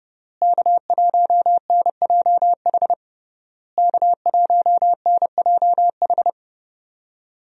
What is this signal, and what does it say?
Morse code, heard as a keyed tone, about 30 wpm, 700 Hz. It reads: K1NJH K1NJH
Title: USA Call Signs - 5 Characters - Repeated 3x - 30wpm